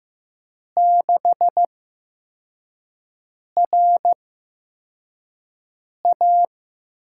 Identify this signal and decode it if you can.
Morse code, heard as a keyed tone, about 16 wpm, spaced as for 4 wpm, 700 Hz. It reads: 6RA
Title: USA Call Signs - 6 Characters - 8x Character Spacing - 15wpm